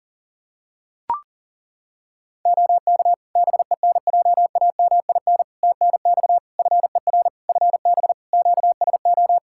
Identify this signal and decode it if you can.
Morse code, heard as a keyed tone, about 40 wpm, 700 Hz. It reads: OK BENJAMIN TNX FER FB QSO
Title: Practice QSO - Exchange 4 - Ending - 40wpm